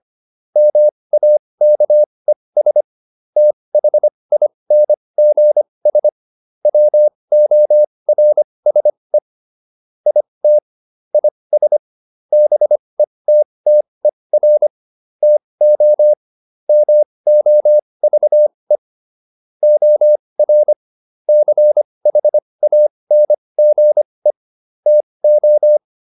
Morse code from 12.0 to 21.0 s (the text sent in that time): BETTER TO MOVE OR